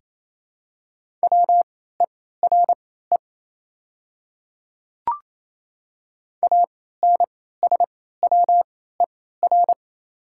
Morse code, heard as a keyed tone, about 27 wpm, spaced as for 16 wpm, 700 Hz. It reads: WERE ANSWER
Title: 3x Character Spacing - Top 300 Words - 28wpm